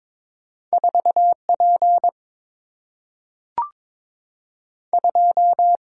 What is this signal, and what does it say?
Morse code, heard as a keyed tone, about 22 wpm, 700 Hz. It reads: 4P 2
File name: practice / 22wpm / Two Letter-Number Combinations - 22wpm